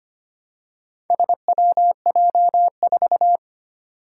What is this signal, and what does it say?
Morse code, heard as a keyed tone, about 25 wpm, 700 Hz. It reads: SWJ4
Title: Four Challenging Letter-Number Combinations 25wpm